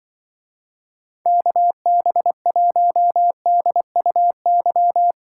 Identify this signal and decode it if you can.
Morse code, heard as a keyed tone, about 25 wpm, 700 Hz. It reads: KB1DUY
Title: Speed Racing - USA 2x3 Call Signs 20wpm